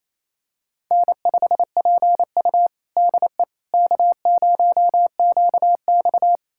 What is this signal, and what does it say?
Morse code, heard as a keyed tone, about 28 wpm, 700 Hz. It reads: N5PU DE K0QX
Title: USA Callsign DE Callsign - 4 Characters - 28wpm